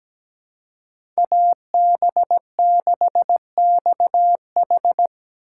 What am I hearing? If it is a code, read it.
Morse code, heard as a keyed tone, about 17 wpm, 700 Hz. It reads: AB6XH
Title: USA Call Signs - 5 Characters - 4x Character Spacing - 17wpm